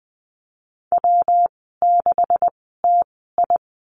Morse code, heard as a keyed tone, about 20 wpm, spaced as for 15 wpm, 700 Hz. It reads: W6TI
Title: 179 Lesson 29g - Slash - Call Signs 20wpm